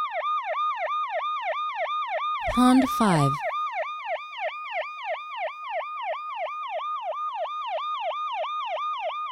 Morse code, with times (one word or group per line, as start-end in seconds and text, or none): none